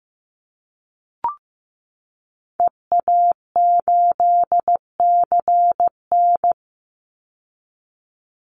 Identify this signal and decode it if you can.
Morse code, heard as a keyed tone, about 15 wpm, 700 Hz. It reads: EA8CN
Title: Call Signs Repeated 2x - 15wpm